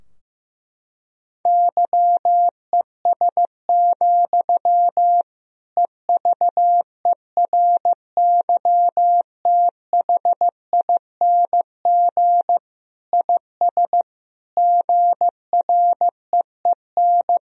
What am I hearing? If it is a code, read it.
Morse code, heard as a keyed tone, about 15 wpm, 700 Hz. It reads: YES, EVERYTHING IS GREEN
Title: Sentences from Top 1000 Words - Review Part 2 - 15wpm